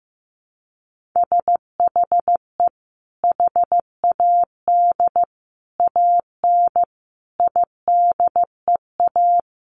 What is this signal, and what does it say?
Morse code, heard as a keyed tone, about 15 wpm, 700 Hz. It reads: SHE HAD AN IDEA
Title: Sentences from Top 1400 Words 15wpm